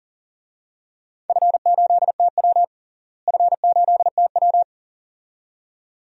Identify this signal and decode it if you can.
Morse code, heard as a keyed tone, about 40 wpm, 700 Hz. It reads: F8TW F8TW
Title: DXCC-100 Call Signs - 4 Characters - Repeated 2x - 40wpm